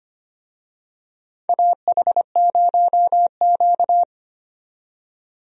Morse code, heard as a keyed tone, about 25 wpm, 700 Hz. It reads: AH0Q
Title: USA Call Signs - 4 Characters - 25wpm